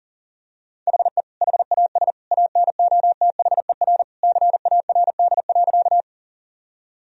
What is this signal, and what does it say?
Morse code, heard as a keyed tone, about 40 wpm, 700 Hz. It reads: HE HAS ANOTHER CARD.
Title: Sentences from Top 1000 Words - Review Part 4 - 40wpm